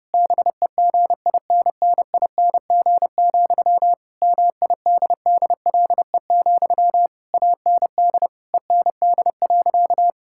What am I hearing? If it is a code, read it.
Morse code, heard as a keyed tone, about 32 wpm, 700 Hz. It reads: BEGINNING, MIDDLE, AND END.
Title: Sentences from Top 1000 Words - Review Part 6 - 30wpm